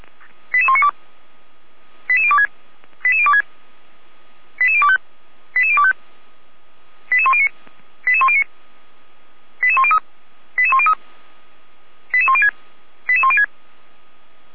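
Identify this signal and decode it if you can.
Five-tone selective call (ZVEI), heard as quick runs of five tones, tones of about 70 ms. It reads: 88192 88026 88026 88025 88025 88109 88109 88103 88103 88106 88106
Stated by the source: ZVEI five-tone selective call 88102 received by SDR, from the Feuerwehr Orlamünde audio server.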